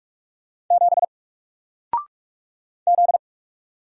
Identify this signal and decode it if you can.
Morse code, heard as a keyed tone, about 44 wpm, 700 Hz. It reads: 7 Z
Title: Single Character (Letter:Number:Punctuation:Prosign) Alt 45wpm